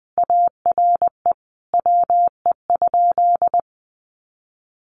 Morse code, heard as a keyed tone, about 20 wpm, 700 Hz. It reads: ARE WE?
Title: Sentences from Top 200 Words 20wpm